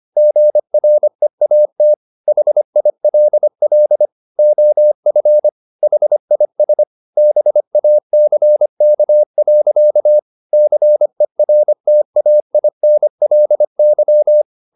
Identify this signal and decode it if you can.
Morse code, heard as a keyed tone, about 25 wpm, 600 Hz. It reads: GREAT HILL OF HIS BACK. CERTAINLY
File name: the old man and the sea part 1 25wpm